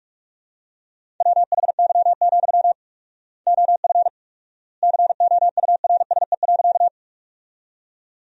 Morse code, heard as a keyed tone, about 46 wpm, 700 Hz. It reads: WHY, OF COURSE.
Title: 4x Word-Spacing - Sentences from Top 500 Words - 45wpm